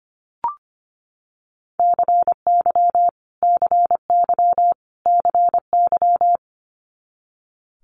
Morse code, heard as a keyed tone, about 25 wpm, 700 Hz. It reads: CY CY CY
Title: Multi-letter Phonemes Repeated 3x 25wpm